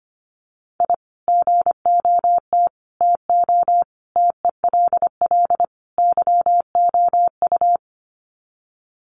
Morse code, heard as a keyed tone, about 25 wpm, 700 Hz. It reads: I GOT TO TELL YOU
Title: Sets of 5 Words Encore2 25wpm